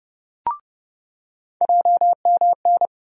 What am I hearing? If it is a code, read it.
Morse code, heard as a keyed tone, about 30 wpm, 700 Hz. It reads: JMN
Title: USA Call Sign Postfixes - 3 Letters 30wpm